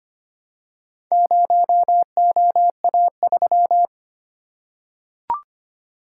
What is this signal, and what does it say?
Morse code, heard as a keyed tone, about 25 wpm, 690 Hz. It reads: 0OA3
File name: practice / 25wpm / Four Letter-Number Combinations - 25wpm